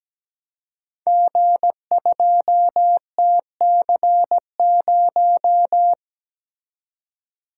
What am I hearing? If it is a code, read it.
Morse code, heard as a keyed tone, about 17 wpm, 700 Hz. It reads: G2TC0
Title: Code Groups - 5 Letter:Numbers - 17wpm